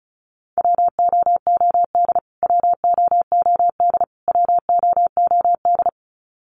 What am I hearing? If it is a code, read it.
Morse code, heard as a keyed tone, about 34 wpm, 700 Hz. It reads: WOOD WOOD WOOD
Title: Top 300 Words Repeated 35wpm